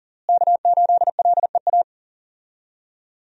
Morse code, heard as a keyed tone, about 40 wpm, 700 Hz. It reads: K8LEA
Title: USA Call Signs - 5 Characters - Repeated 3x - 40wpm